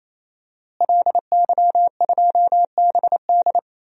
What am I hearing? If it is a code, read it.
Morse code, heard as a keyed tone, about 27 wpm, 700 Hz. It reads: LY2BD
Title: DXCC-100 Call Signs - 5 Characters - Repeated 3x - 28wpm